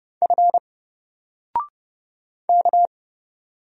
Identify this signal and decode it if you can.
Morse code, heard as a keyed tone, about 30 wpm, 700 Hz. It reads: F K
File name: Single-Letter-Number-30wpm